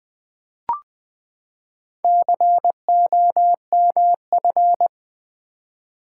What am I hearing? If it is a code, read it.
Morse code, heard as a keyed tone, about 20 wpm, 700 Hz. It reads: COMF
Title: Build A Word - 4 Letters At A Time - 11-Letter Words - 20wpm